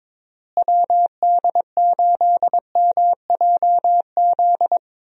Morse code, heard as a keyed tone, about 22 wpm, 700 Hz. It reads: WD8MJZ
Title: USA Call Signs - 6 Characters - Repeated 3x - 22wpm